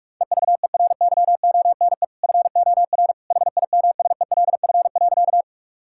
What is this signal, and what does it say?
Morse code, heard as a keyed tone, about 46 wpm, 700 Hz. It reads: EVERYONE FOR HIMSELF.
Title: Sentences from Top 1600 Words 45wpm